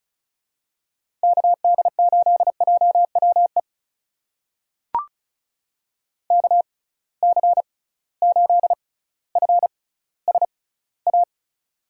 Morse code, heard as a keyed tone, about 36 wpm, spaced as for 18 wpm, 700 Hz. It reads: KD8JWE KC8FSA
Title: USA Call Signs - 6 Characters - 6x Character Spacing - 35wpm